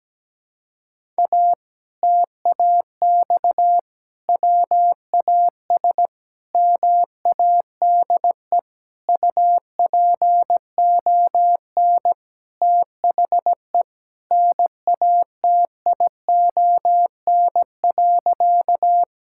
Morse code, heard as a keyed tone, about 17 wpm, 700 Hz. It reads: A TAX WAS MADE UPON THE NATION.
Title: Sentences from Top 1000 Words - Review Part 6 - 17wpm